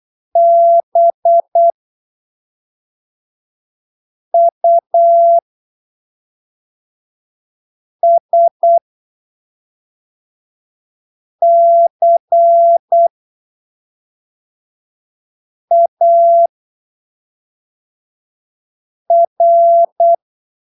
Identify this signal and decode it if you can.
Morse code, heard as a keyed tone, about 8 wpm, spaced as for 3 wpm, 680 Hz. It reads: BUSCAR
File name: arquivos / texto / texto 15 08ppm